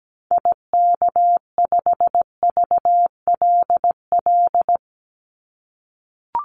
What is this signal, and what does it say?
Morse code, heard as a keyed tone, about 17 wpm, 700 Hz. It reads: IK5VLL
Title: Call Signs - 17wpm